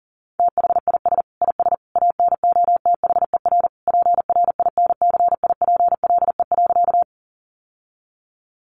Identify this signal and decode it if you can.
Morse code, heard as a keyed tone, about 40 wpm, 700 Hz. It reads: THIS IS ANOTHER PRINCIPLE.
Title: Sentences from Top 1400 Words 40wpm